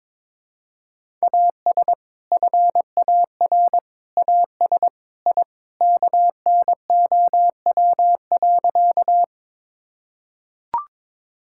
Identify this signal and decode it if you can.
Morse code, heard as a keyed tone, about 22 wpm, 700 Hz. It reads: AS FAR AS I KNOW.
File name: practice / 22wpm / Sentences with 3-Word Phrases - 22wpm